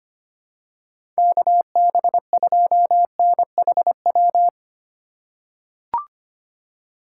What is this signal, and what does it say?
Morse code, heard as a keyed tone, about 25 wpm, 700 Hz. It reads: KB2NHW
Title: USA Call Signs - 6 Characters - 3x Character Spacing - 25wpm